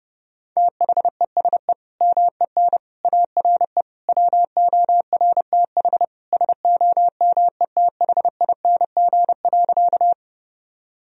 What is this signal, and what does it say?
Morse code, heard as a keyed tone, about 28 wpm, 700 Hz. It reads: THESE MEN ARE WORTH SOMETHING.
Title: Sentences from Top 1000 Words - Review Part 1 - 30wpm